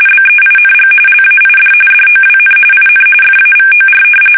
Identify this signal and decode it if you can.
RTTY, about 45.45 baud. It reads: BEING LAID DOWN IN FAVOR
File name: fsk850 / battleships